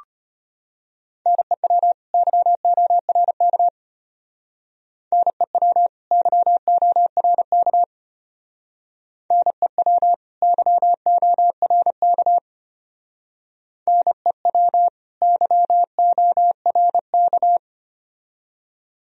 Morse code, heard as a keyed tone, about 30 wpm, 700 Hz. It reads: NEW YORK NEW YORK NEW YORK NEW YORK
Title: Speed-Racing - Sets of 2 Words 25wpm